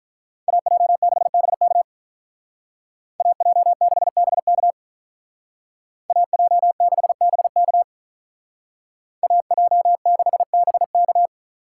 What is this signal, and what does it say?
Morse code, heard as a keyed tone, about 46 wpm, 700 Hz. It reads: AJ6BK AJ6BK AJ6BK AJ6BK